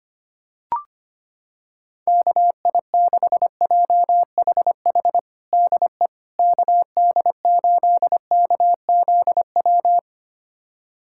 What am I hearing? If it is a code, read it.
Morse code, heard as a keyed tone, about 25 wpm, 700 Hz. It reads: KI6JHH DE KD8KZW
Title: USA Callsign DE Callsign 25wpm